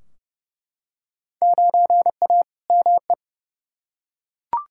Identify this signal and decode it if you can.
Morse code, heard as a keyed tone, about 30 wpm, 700 Hz. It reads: 9A ME
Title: Field Day Exchanges Encore - 30wpm